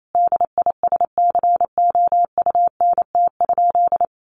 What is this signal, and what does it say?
Morse code, heard as a keyed tone, about 28 wpm, 700 Hz. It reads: DISCOUNT?
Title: Sentences from Top 1200 Words 28wpm